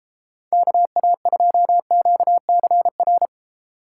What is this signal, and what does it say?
Morse code, heard as a keyed tone, about 33 wpm, 700 Hz. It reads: KA2QCR